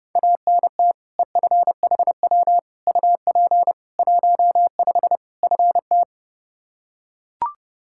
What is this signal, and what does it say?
Morse code, heard as a keyed tone, about 30 wpm, 700 Hz. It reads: ANT EFHW UP 15 FT
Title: Practice QSO - Exchange 2 - Info - 30wpm